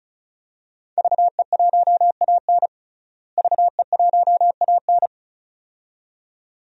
Morse code, heard as a keyed tone, about 34 wpm, 700 Hz. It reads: VE1AN VE1AN
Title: DXCC-100 Call Signs - 5 Characters - Repeated 2x - 35wpm